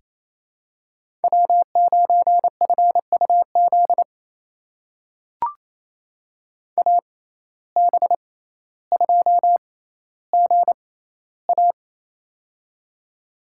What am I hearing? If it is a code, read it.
Morse code, heard as a keyed tone, about 28 wpm, spaced as for 14 wpm, 700 Hz. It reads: W9FUZ AB2GA
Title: USA Call Signs - 5 Characters - 6x Character Spacing - 28wpm